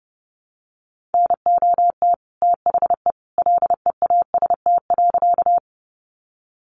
Morse code, heard as a keyed tone, about 30 wpm, 700 Hz. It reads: NOT THE LEAST.
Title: Sentences from Top 700 Words 30wpm